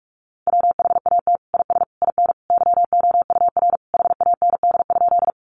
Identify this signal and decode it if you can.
Morse code, heard as a keyed tone, about 44 wpm, 700 Hz. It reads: WHAT IS IN YOUR HAND?
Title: Sentences from Top 200 Words 45wpm